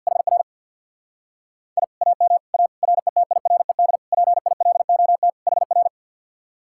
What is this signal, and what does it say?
Morse code, heard as a keyed tone, about 50 wpm, 700 Hz. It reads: HR I AM A RETIRED PILOT HR
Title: Speed-Racing Common QSO Phrases Encore - 50wpm